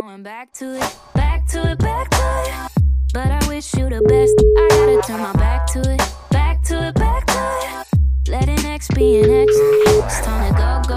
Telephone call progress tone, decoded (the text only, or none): ringback tone